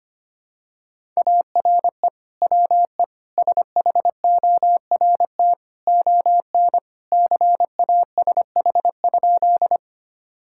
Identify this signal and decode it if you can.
Morse code, heard as a keyed tone, about 25 wpm, 700 Hz. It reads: ARE WE SHORT ON CASH?
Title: Sentences from Top 1000 Words - Review Part 5 - 25wpm